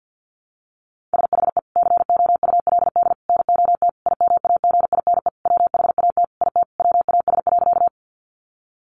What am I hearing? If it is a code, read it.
Morse code, heard as a keyed tone, about 50 wpm, 700 Hz. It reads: SHE COULD NOT IMAGINE WHAT IT WAS.
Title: Sentences from Top 1400 Words 50wpm